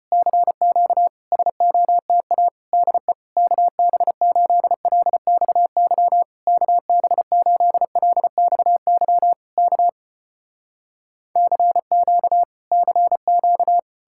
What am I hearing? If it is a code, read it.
Morse code, heard as a keyed tone, about 35 wpm, 700 Hz. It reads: CQ SOTA DE KB8LXY KB8LXY K CQ CQ